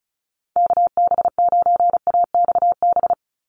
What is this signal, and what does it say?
Morse code, heard as a keyed tone, about 34 wpm, 700 Hz. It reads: KB9AXB
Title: USA Call Signs 35wpm